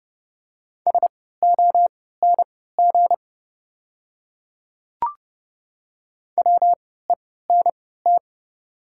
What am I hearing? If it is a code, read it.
Morse code, heard as a keyed tone, about 31 wpm, spaced as for 17 wpm, 700 Hz. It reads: SONG WENT
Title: 3x Character Spacing - Top 500 Words - 30wpm